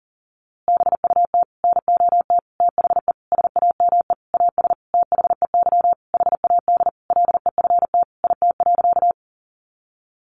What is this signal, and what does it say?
Morse code, heard as a keyed tone, about 40 wpm, 700 Hz. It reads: BUT NOT THE SAME AS THEY HAD LEFT IT.
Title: A Sound Of Thunder 40wpm